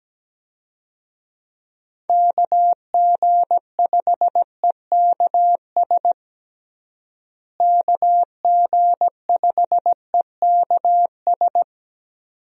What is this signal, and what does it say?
Morse code, heard as a keyed tone, about 17 wpm, 700 Hz. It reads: KG5EKS KG5EKS
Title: USA Call Signs - 6 Characters - Repeated 2x - 17wpm